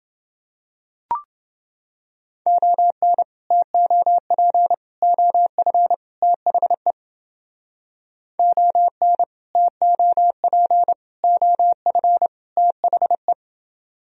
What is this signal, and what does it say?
Morse code, heard as a keyed tone, about 31 wpm, 700 Hz. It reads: ON TOP OF THE ON TOP OF THE